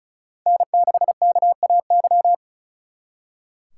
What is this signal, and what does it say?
Morse code, heard as a keyed tone, about 42 wpm, 700 Hz. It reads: N6KAY
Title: Speed Racing - USA Call Signs 35wpm